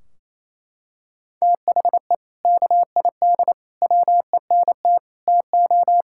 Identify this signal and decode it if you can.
Morse code, heard as a keyed tone, about 28 wpm, 700 Hz. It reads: THE KID WENT TO
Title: Sentences from Top 1600 Words 28wpm